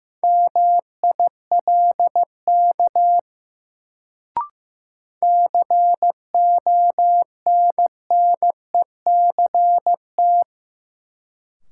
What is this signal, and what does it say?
Morse code, heard as a keyed tone, about 15 wpm, 700 Hz. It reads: MILK CONNECT
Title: Sentences from Top 1500 Words - New Word Drill Alt - 15wpm